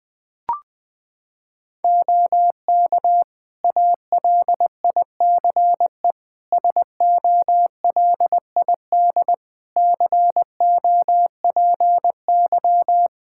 Morse code, heard as a keyed tone, about 20 wpm, 700 Hz. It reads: OK ALICE SOLID COPY
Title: Practice QSO - Exchange 3 - More Info - 20wpm